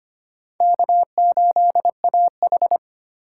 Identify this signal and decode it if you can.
Morse code, heard as a keyed tone, about 34 wpm, spaced as for 30 wpm, 700 Hz. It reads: K8AH
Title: Speed Racing - USA 1x2 Call Signs 25wpm